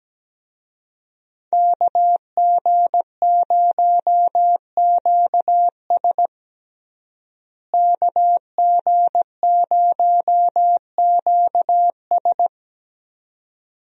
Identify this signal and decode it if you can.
Morse code, heard as a keyed tone, about 17 wpm, 700 Hz. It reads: KG0QS KG0QS
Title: USA Call Signs - 5 Characters - Repeated 2x - 17wpm